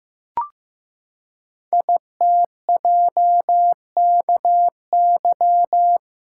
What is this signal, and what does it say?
Morse code, heard as a keyed tone, about 15 wpm, 700 Hz. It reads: ITJKY
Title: Code Groups - 5 Letters - 15wpm